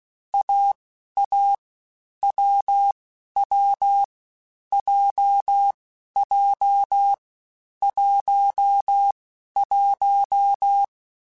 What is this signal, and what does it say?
Morse code, heard as a keyed tone, about 16 wpm, spaced as for 11 wpm, 800 Hz. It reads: AA WW JJ 11
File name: Difficult Combos / Difficult Combos-mp3s / AWJ1 @15